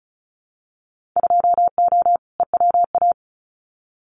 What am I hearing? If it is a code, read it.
Morse code, heard as a keyed tone, about 34 wpm, 700 Hz. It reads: 2O EWA
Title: Winter Field Day Exchanges Encore 35wpm